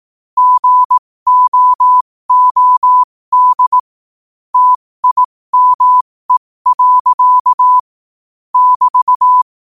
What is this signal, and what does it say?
Morse code, heard as a keyed tone, about 18 wpm, spaced as for 15 wpm, 1000 Hz. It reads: GOOD TIME. <BT>
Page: 15 WPM morse code quotes